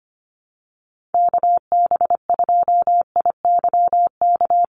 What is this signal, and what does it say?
Morse code, heard as a keyed tone, about 25 wpm, 700 Hz. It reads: KB2IYK
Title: USA Call Signs 25wpm